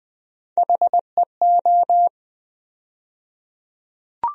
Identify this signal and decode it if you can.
Morse code, heard as a keyed tone, about 20 wpm, 700 Hz. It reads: HEO
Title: USA Call Sign Postfixes - 3 Letters 20wpm